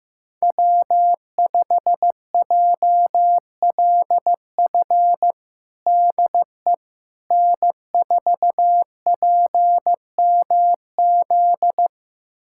Morse code, heard as a keyed tone, about 15 wpm, 700 Hz. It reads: W5JLF DE N4PMZ
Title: USA Callsign DE Callsign - 5 Characters - 15wpm